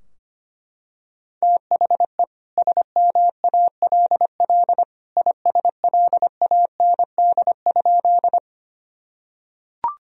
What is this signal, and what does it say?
Morse code, heard as a keyed tone, about 25 wpm, 700 Hz. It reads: THE SMALL ISLAND?
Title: Sentences from Top 1000 Words - Review Part 3 - 25wpm